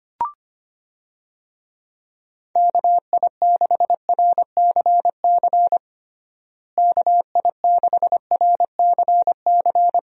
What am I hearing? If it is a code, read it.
Morse code, heard as a keyed tone, about 25 wpm, 700 Hz. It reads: KI6RCC KI6RCC
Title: USA Call Signs - 6 Characters - Repeated 2x - 25wpm